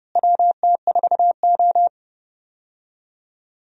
Morse code, heard as a keyed tone, about 32 wpm, 700 Hz. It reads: WT4O
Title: USA Call Signs - 4 Characters - Repeated 2x - 30wpm